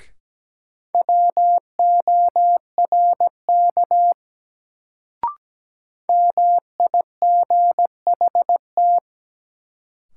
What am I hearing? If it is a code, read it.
Morse code, heard as a keyed tone, about 17 wpm, 700 Hz. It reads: WORK MIGHT
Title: Sets of 2 Words - New Word Drill - 17wpm